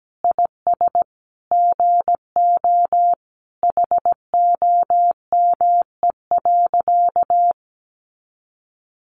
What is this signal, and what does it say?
Morse code, heard as a keyed tone, about 17 wpm, 700 Hz. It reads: IS GO HOME.
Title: Sentences from Top 200 Words 17wpm